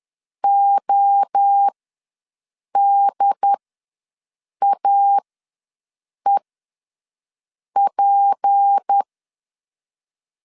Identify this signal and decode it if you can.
Morse code, heard as a keyed tone, about 10 wpm, spaced as for 6 wpm, 800 Hz. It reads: ODAEP
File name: LEZIONI / 7 GF9Z / 7 GF ws 10wpm 3918